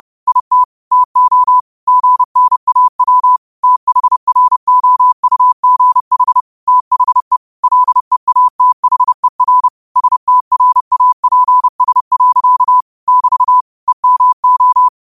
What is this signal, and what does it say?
Morse code, heard as a keyed tone, about 30 wpm, 1000 Hz. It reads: IT TO GNAW THROUGH THE LEATHER STRAPS. <BT> EMO
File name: qotdfiles / 30wpm